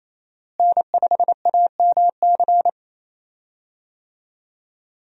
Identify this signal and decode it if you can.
Morse code, heard as a keyed tone, about 27 wpm, 700 Hz. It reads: N5AMC
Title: USA Call Signs - 5 Characters - Repeated 2x - 28wpm